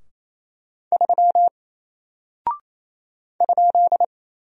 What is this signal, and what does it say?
Morse code, heard as a keyed tone, about 28 wpm, 700 Hz. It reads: 3 ?
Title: Single Character (Letter:Number:Punctuation:Prosign) Alt 28wpm